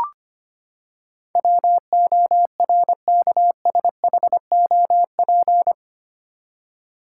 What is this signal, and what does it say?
Morse code, heard as a keyed tone, about 25 wpm, 700 Hz. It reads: WORKSHOP
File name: Eight Letter Words - 25wpm